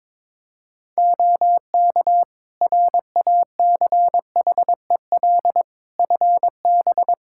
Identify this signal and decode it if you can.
Morse code, heard as a keyed tone, about 22 wpm, 700 Hz. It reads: OK RACHEL FB
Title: Practice QSO - Exchange 2 - Info - 22wpm